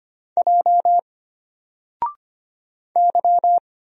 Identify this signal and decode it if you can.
Morse code, heard as a keyed tone, about 25 wpm, 700 Hz. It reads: J Y
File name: Single-Letter-Number-25wpm